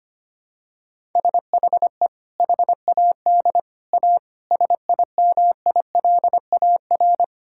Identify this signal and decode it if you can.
Morse code, heard as a keyed tone, about 25 wpm, 700 Hz. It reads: SHE HAD A SIMILAR
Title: Sentences from Top 1000 Words - Review Part 5 - 25wpm